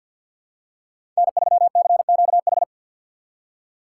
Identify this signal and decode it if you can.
Morse code, heard as a keyed tone, about 48 wpm, 700 Hz. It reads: N3CQH
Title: USA Call Signs - 5 Characters - 3x Character Spacing - 50wpm